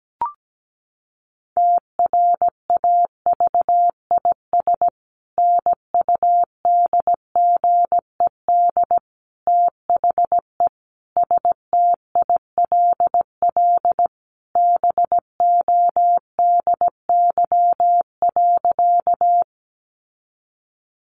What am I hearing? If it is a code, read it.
Morse code, heard as a keyed tone, about 17 wpm, 700 Hz. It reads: TRAVIS NUDGED THE STILL BODY.
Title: A Sound Of Thunder 17wpm